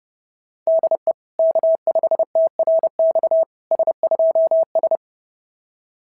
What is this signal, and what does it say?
Morse code, heard as a keyed tone, about 28 wpm, 660 Hz. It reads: DE K5TRX S2S
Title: SOTA QSO Practice 30wpm